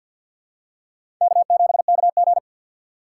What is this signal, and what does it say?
Morse code, heard as a keyed tone, about 45 wpm, 690 Hz. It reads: K7KG
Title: USA Call Signs - 4x Character Spacing - 50wpm